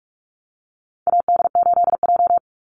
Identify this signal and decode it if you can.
Morse code, heard as a keyed tone, about 44 wpm, 700 Hz. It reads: AD8J
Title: Call Signs - 45wpm